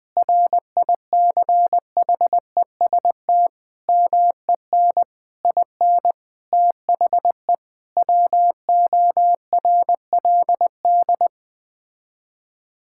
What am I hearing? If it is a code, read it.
Morse code, heard as a keyed tone, about 20 wpm, 700 Hz. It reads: RICHEST MEN IN THE WORLD
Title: Sets of 8 Words - 20wpm